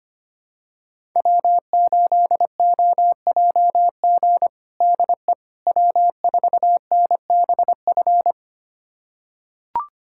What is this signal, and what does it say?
Morse code, heard as a keyed tone, about 25 wpm, 700 Hz. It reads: W8OJG DE W4NBF
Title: USA Callsign DE Callsign - 5 Characters - 25wpm